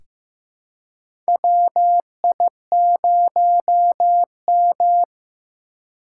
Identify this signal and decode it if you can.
Morse code, heard as a keyed tone, about 16 wpm, 700 Hz. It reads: WI0M
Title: USA Call Signs - 4 Characters - 2x Character Spacing - 15wpm